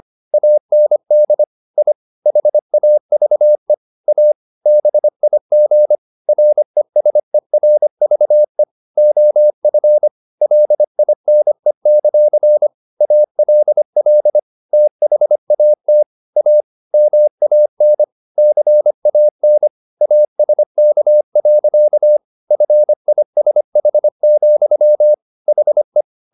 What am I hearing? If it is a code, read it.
Morse code, heard as a keyed tone, about 25 wpm, 600 Hz. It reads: AND I HAVE A BIG RESERVE OF LINE; ALL THAT A MAN CAN ASK. FISH, HE
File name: the old man and the sea part 1 25wpm